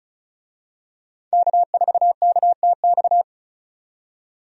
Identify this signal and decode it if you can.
Morse code, heard as a keyed tone, about 36 wpm, 700 Hz. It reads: K4KTX
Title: USA Call Signs - 5 Characters - 35wpm